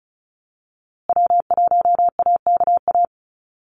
Morse code, heard as a keyed tone, about 34 wpm, 700 Hz. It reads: W1AKA
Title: USA Call Signs Repeated 3x 35wpm